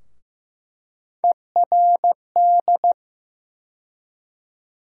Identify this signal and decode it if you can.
Morse code, heard as a keyed tone, about 15 wpm, 700 Hz. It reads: ERD
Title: USA Call Sign Postfixes - 3 Letters 15wpm